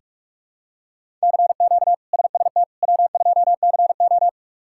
Morse code, heard as a keyed tone, about 44 wpm, 700 Hz. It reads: CQ SST W2CO
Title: K1USN SST QSO Practice - 45wpm